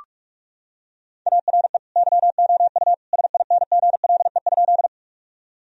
Morse code, heard as a keyed tone, about 43 wpm, 700 Hz. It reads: ARE YOU SINGLE?
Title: Sentences from Top 1000 Words - Review Part 4 - 45wpm